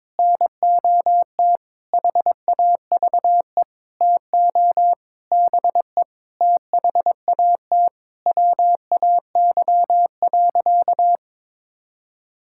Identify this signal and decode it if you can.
Morse code, heard as a keyed tone, about 22 wpm, 700 Hz. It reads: NOT HAVE TO BE THAT WAY.
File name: practice / 22wpm / Sentences with 8-Word Phrases - 22wpm